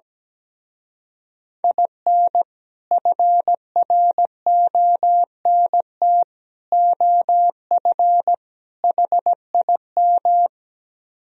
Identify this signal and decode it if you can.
Morse code, heard as a keyed tone, about 17 wpm, 700 Hz. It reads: IN FRONT OF HIM
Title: Speed-Racing - Sets of 4 Words 17wpm